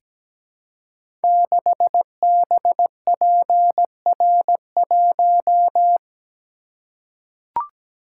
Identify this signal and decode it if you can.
Morse code, heard as a keyed tone, about 17 wpm, 700 Hz. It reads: 6BPR1
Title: Code Groups - 5 Letter:Numbers - 17wpm